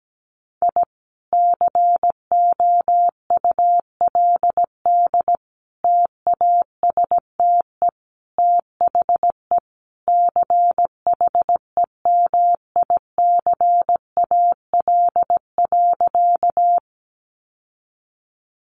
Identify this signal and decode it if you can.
Morse code, heard as a keyed tone, about 17 wpm, 700 Hz. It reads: I COULD TASTE THE CHEMICAL.
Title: Sentences from Top 1200 Words 17wpm